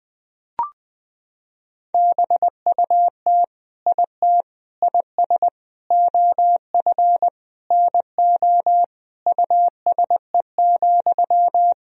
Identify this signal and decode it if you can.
Morse code, heard as a keyed tone, about 20 wpm, 700 Hz. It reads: BUT IT IS OF NO USE,
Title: Sentences from Top 1000 Words - Review Part 2 - 20wpm